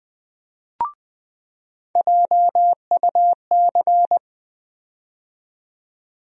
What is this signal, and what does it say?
Morse code, heard as a keyed tone, about 20 wpm, 700 Hz. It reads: JUC
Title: USA Call Sign Postfixes - 3 Letters 20wpm